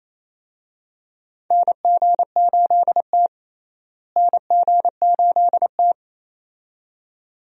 Morse code, heard as a keyed tone, about 27 wpm, 700 Hz. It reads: NG8T NG8T
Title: USA Call Signs - 4 Characters - Repeated 2x - 28wpm